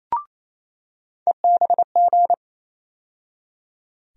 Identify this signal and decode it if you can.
Morse code, heard as a keyed tone, about 28 wpm, 700 Hz. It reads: EBG
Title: USA Call Sign Postfixes - 3 Letters 28wpm